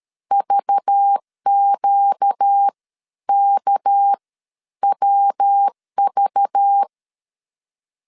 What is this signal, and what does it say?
Morse code, heard as a keyed tone, about 13 wpm, spaced as for 10 wpm, 800 Hz. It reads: VQKWV
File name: CW / DET / 1 6 1500C S A 12wpm 3440